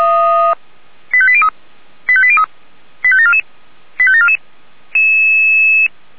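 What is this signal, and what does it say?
Five-tone selective call (ZVEI), heard as quick runs of five tones, tones of about 70 ms. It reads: 85803 85803 85744 85744
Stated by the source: ZVEI five-tone selective call 85719 received by SDR.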